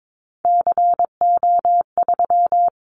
Morse code, heard as a keyed tone, about 22 wpm, 700 Hz. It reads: CO3
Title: Callsign Prefixes for Top 100 DXCC - 22wpm